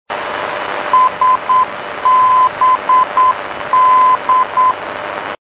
Morse code, heard as a keyed tone, about 9 wpm, 1020 Hz, with noise. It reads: SBD
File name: Registros de radio archivos / NDB SBD